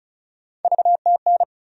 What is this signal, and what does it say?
Morse code, heard as a keyed tone, about 34 wpm, 700 Hz. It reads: VTN
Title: USA Call Sign Postfixes - 3 Letters 35wpm